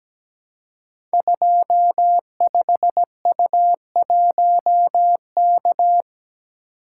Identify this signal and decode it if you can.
Morse code, heard as a keyed tone, about 17 wpm, 700 Hz. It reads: 25U1K
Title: Code Groups - 5 Letter:Numbers - 17wpm